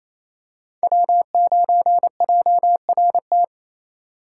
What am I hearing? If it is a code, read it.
Morse code, heard as a keyed tone, about 28 wpm, 700 Hz. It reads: W9JRT
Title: USA Call Signs - 5 Characters - 4x Character Spacing - 28wpm